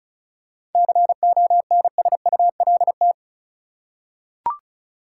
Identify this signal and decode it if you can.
Morse code, heard as a keyed tone, about 34 wpm, 700 Hz. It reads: CONSULT
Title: Seven Letter Words - 35wpm